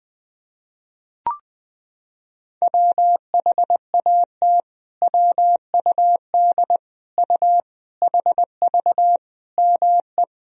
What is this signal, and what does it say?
Morse code, heard as a keyed tone, about 20 wpm, 700 Hz. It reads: WHAT WUD U HV ME
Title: Sentences from Top 200 Words and Abbreviations 20wpm